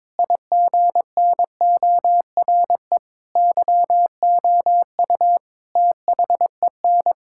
Morse code, heard as a keyed tone, about 22 wpm, 700 Hz. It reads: IGNORE YOU THEN
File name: practice / 22wpm / Sets of 8 Words - 22wpm